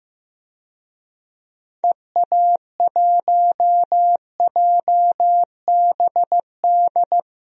Morse code, heard as a keyed tone, about 16 wpm, 700 Hz. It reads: EA1JBD
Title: DXCC-100 Call Signs - 6 Characters - Repeated 2x - 15wpm